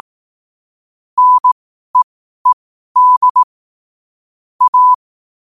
Morse code, heard as a keyed tone, about 18 wpm, spaced as for 12 wpm, 1000 Hz. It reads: NEED A
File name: qotdfiles / 12wpm